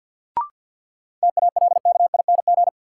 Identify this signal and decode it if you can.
Morse code, heard as a keyed tone, about 53 wpm, 700 Hz. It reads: TALKING
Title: Sets of 2 Words - New Word Drill - 50wpm